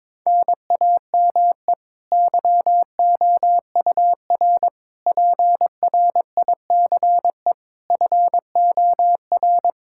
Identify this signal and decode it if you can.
Morse code, heard as a keyed tone, about 21 wpm, 700 Hz. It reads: NAME YOUR PRICE FOR
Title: Sentences from Top 1000 Words - Review Part 5 - 22wpm